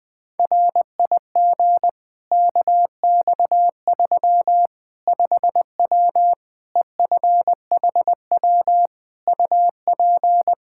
Morse code, heard as a keyed tone, about 20 wpm, 700 Hz. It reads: RIG KX3 5W EFHW UP
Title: Practice QSO - Exchange 2 - Info - 20wpm